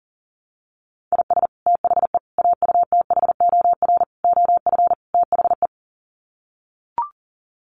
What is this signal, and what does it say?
Morse code, heard as a keyed tone, about 40 wpm, 700 Hz. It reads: IS THE AUTHOR OF THE
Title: Sets of 5 Words Encore 40wpm